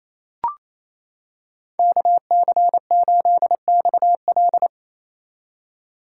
Morse code, heard as a keyed tone, about 28 wpm, 700 Hz. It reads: KC8XL
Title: Code Groups - 5 Letter:Numbers - 28wpm